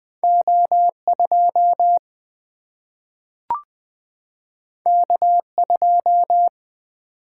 Morse code, heard as a keyed tone, about 20 wpm, 700 Hz. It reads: O2 K2
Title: Single-Letter Single-Number 20wpm